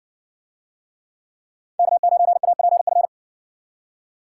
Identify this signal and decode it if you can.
Morse code, heard as a keyed tone, about 60 wpm, 700 Hz. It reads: K9APF